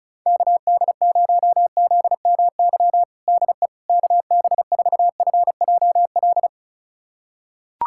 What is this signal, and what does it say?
Morse code, heard as a keyed tone, about 35 wpm, 700 Hz. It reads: KD0ZMY DE KB4FJL
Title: USA Callsign DE Callsign - 6 Characters - 35wpm